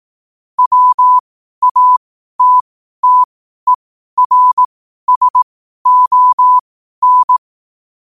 Morse code, heard as a keyed tone, about 18 wpm, spaced as for 12 wpm, 1000 Hz. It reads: WATTERSON
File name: qotdfiles / 12wpm